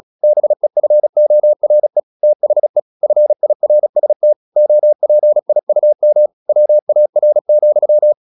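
Morse code, heard as a keyed tone, about 36 wpm, 600 Hz. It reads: BEFORE THE FIRST OPIUM WAR,
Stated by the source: Morse practice at 36 wpm, text of Ken Follett's novel Never, from CW med Gnister.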